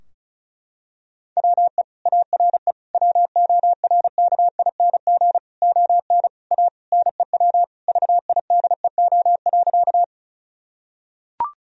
Morse code, heard as a keyed tone, about 36 wpm, 700 Hz. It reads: WE ARE WORKING ON A NEW VIDEO.
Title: Sentences from Top 1000 Words - Review Part 4 - 35wpm